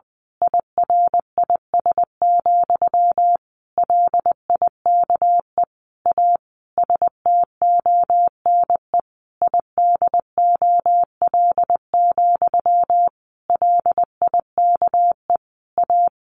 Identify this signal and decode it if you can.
Morse code, heard as a keyed tone, about 20 wpm, 700 Hz. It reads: IRIS, LIKE A STONE IDOL, LIKE A
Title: A Sound Of Thunder 20wpm